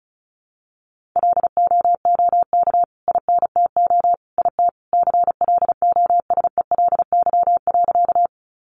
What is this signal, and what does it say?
Morse code, heard as a keyed tone, about 34 wpm, 700 Hz. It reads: LOOK INTO IT CLOSELY.
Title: Sentences from Top 1400 Words 35wpm